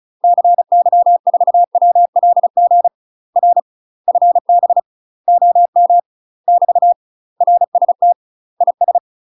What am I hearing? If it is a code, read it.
Morse code, heard as a keyed tone, about 36 wpm, 700 Hz. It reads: CY4WLG R FB OM <BT> RST IS
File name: UK MASTER QSOs TRAINING / 35WPM / UK MASTER QSOs 0033